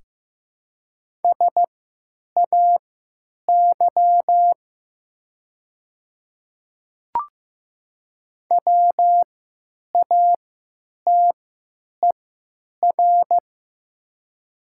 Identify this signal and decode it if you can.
Morse code, heard as a keyed tone, about 16 wpm, spaced as for 9 wpm, 700 Hz. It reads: SAY WATER
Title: 3x Character Spacing - Top 200 Words - 15wpm